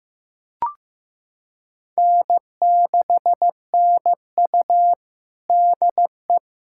Morse code, heard as a keyed tone, about 15 wpm, 700 Hz. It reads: N6NU DE
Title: USA Callsign DE Callsign - 4 Characters - 15wpm